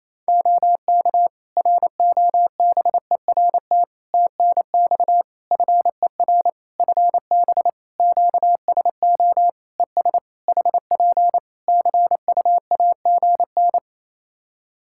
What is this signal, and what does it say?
Morse code, heard as a keyed tone, about 27 wpm, 700 Hz. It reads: OK ROBERT TNX FER FB QSO ES HP CUAGN
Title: K1USN SST QSO Practice 28wpm